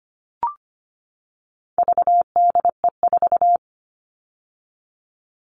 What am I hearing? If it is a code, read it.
Morse code, heard as a keyed tone, about 25 wpm, 700 Hz. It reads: VDE4
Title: All Cut Numbers - 4-Digit 25wpm